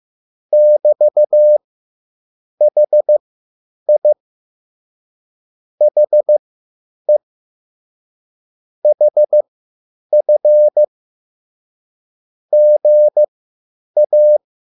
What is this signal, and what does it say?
Morse code, heard as a keyed tone, about 15 wpm, spaced as for 8 wpm, 600 Hz. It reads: <BT>HI HE HF GA
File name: QSL / CW lesson